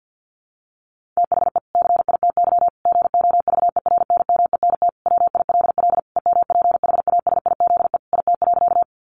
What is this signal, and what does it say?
Morse code, heard as a keyed tone, about 50 wpm, 700 Hz. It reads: THE CITY GOVERNMENT WILL EMPHASIZE IT.
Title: Sentences from Top 1500 Words 50wpm